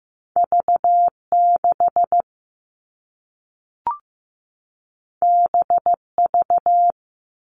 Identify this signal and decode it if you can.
Morse code, heard as a keyed tone, about 15 wpm, 700 Hz. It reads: V6 BV
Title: All Cut Numbers - 2-Digit 15wpm